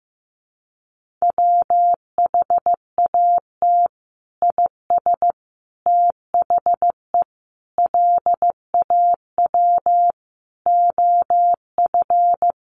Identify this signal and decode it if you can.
Morse code, heard as a keyed tone, about 14 wpm, 700 Hz. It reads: WHAT IS THE LAW OF
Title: Sentences from Top 1300 Words 15wpm